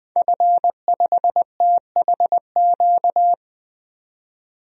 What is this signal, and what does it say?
Morse code, heard as a keyed tone, about 21 wpm, 700 Hz. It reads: F5THQ
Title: DXCC-100 Call Signs - 5 Characters - 20wpm